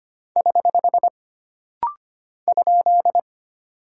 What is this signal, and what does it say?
Morse code, heard as a keyed tone, about 25 wpm, 700 Hz. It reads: <HH> ?
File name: Single Punctuation-Prosign - Rapid-Fire - 25wpm